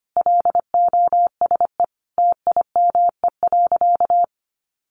Morse code, heard as a keyed tone, about 25 wpm, 700 Hz. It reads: LOSE TIME.
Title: Sentences from Top 1100 Words 25wpm